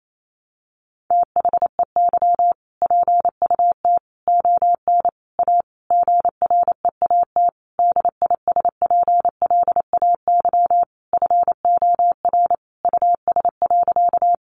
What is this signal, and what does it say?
Morse code, heard as a keyed tone, about 28 wpm, 700 Hz. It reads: THEY PUT ON A GREAT DISPLAY FOR US.
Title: Sentences from Top 1200 Words 28wpm